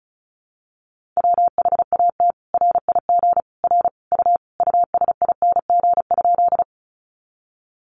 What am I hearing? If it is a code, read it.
Morse code, heard as a keyed tone, about 34 wpm, 700 Hz. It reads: WHAT RIG R U USING?
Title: Common QSO Phrases 35wpm